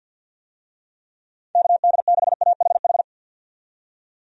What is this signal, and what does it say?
Morse code, heard as a keyed tone, about 51 wpm, 700 Hz. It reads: KD6AHH
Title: USA Call Signs - 6 Characters - 50wpm